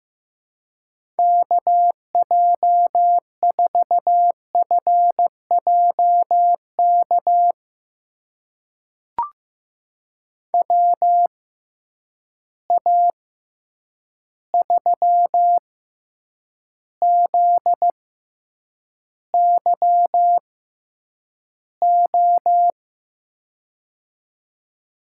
Morse code, heard as a keyed tone, about 15 wpm, spaced as for 8 wpm, 700 Hz. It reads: KJ4FJK WA3ZYO
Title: USA Call Signs - 6 Characters - 6x Character Spacing - 15wpm